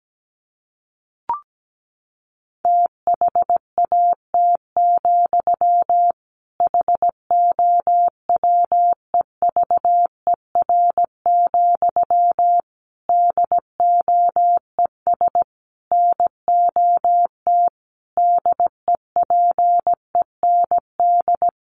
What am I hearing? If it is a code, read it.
Morse code, heard as a keyed tone, about 17 wpm, 700 Hz. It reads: THAT, HOWEVER, DOES NOT DEPEND
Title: Sentences from Top 1300 Words 17wpm